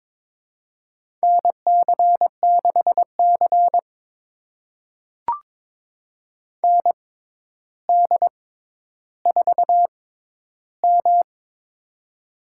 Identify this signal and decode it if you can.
Morse code, heard as a keyed tone, about 22 wpm, spaced as for 11 wpm, 700 Hz. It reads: NC6C ND4M
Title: USA Call Signs - 6x Character Spacing - 22wpm